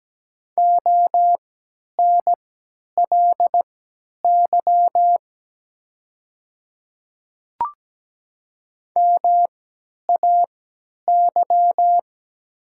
Morse code, heard as a keyed tone, about 17 wpm, spaced as for 10 wpm, 690 Hz. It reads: ONLY MAY
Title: 3x Character Spacing - Top 200 Words - 17wpm